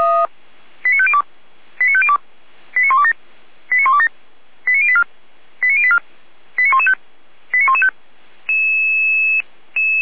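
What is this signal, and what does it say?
Five-tone selective call (ZVEI), heard as quick runs of five tones, tones of about 70 ms. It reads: 89502 89502 89127 89127 89084 89084 89115 89115
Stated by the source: ZVEI five-tone selective call received by SDR, from the Feuerwehr Orlamünde fire brigade audio server.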